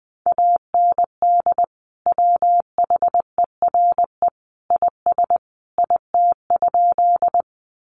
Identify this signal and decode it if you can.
Morse code, heard as a keyed tone, about 20 wpm, 700 Hz. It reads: AND WHERE IS IT?
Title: Sentences from Top 200 Words 20wpm